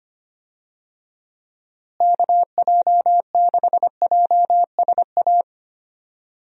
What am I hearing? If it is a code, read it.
Morse code, heard as a keyed tone, about 25 wpm, 700 Hz. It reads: KJ6JSA
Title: USA Call Signs - 6 Characters - Repeated 3x - 25wpm